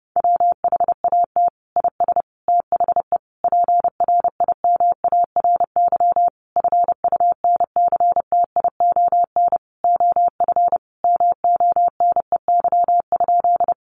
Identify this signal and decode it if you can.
Morse code, heard as a keyed tone, about 30 wpm, 700 Hz. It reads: WHAT IS THE PRIMARY FUNCTION OF MONEY?
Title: Sentences from Top 1100 Words 30wpm